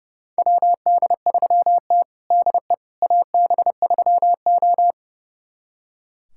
Morse code, heard as a keyed tone, about 32 wpm, 700 Hz. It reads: WD3T DE AB3O
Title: USA Callsign DE Callsign - 4 Characters - 30wpm